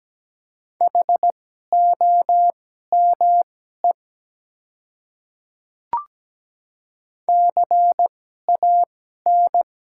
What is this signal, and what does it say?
Morse code, heard as a keyed tone, about 17 wpm, spaced as for 12 wpm, 700 Hz. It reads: HOME CAN
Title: 2x Character Spacing - Top 200 Words - 17wpm